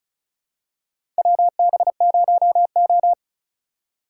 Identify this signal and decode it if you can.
Morse code, heard as a keyed tone, about 34 wpm, 700 Hz. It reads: WB0O
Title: USA Call Signs - 8x Character Spacing - 35wpm